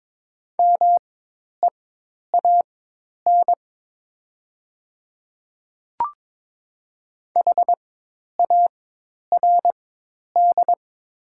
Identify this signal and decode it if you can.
Morse code, heard as a keyed tone, about 22 wpm, spaced as for 10 wpm, 700 Hz. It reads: MEAN HARD
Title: 4x Character Spacing - Top 300 Words - 22wpm